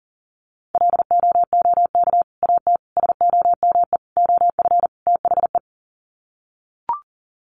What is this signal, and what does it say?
Morse code, heard as a keyed tone, about 40 wpm, 700 Hz. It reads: LOOK AT SOME OF THE
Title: Sets of 5 Words Encore 40wpm